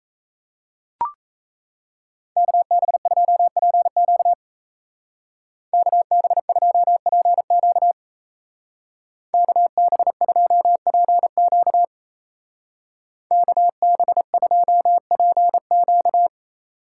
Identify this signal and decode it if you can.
Morse code, heard as a keyed tone, about 34 wpm, 700 Hz. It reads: KB2PQ KB2PQ KB2PQ KB2PQ
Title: Speed Racing - USA 2x2 Call Signs 28wpm